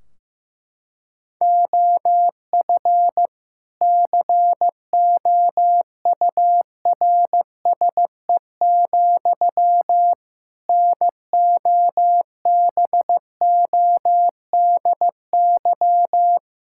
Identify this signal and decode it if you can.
Morse code, heard as a keyed tone, about 15 wpm, 700 Hz. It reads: OF COURSE, NOBODY
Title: Sentences from Top 1600 Words 15wpm